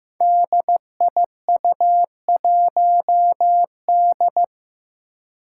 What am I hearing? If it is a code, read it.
Morse code, heard as a keyed tone, about 15 wpm, 700 Hz. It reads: DIU1D
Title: Five Challenging Letter-Number Combinations 15wpm